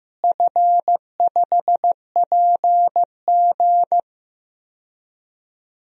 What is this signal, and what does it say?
Morse code, heard as a keyed tone, about 15 wpm, 700 Hz. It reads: F5PG
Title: DXCC-100 Call Signs - 4 Characters - Repeated 2x - 15wpm